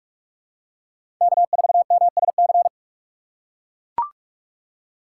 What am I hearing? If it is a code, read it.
Morse code, heard as a keyed tone, about 43 wpm, 700 Hz. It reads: K 4 M S C
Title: USA Call Signs - 5 Characters - 6x Character Spacing - 45wpm